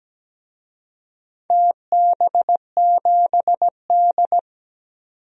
Morse code, heard as a keyed tone, about 17 wpm, 700 Hz. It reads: TB7D
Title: DXCC-100 Call Signs - 4 Characters - 17wpm